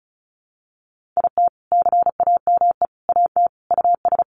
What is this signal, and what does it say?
Morse code, heard as a keyed tone, about 34 wpm, 700 Hz. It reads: IT CAME AT US
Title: Sentences from Top 1400 Words 35wpm